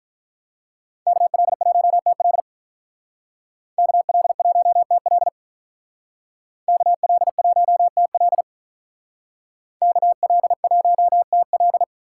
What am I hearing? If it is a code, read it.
Morse code, heard as a keyed tone, about 46 wpm, 700 Hz. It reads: KL1TL KL1TL KL1TL KL1TL